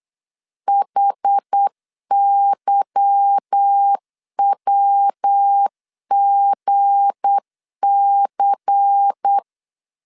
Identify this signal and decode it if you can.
Morse code, heard as a keyed tone, about 8 wpm, 800 Hz. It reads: HYWGC
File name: LEZIONI / 8 YJ45 / 8 Y 8wpm 3245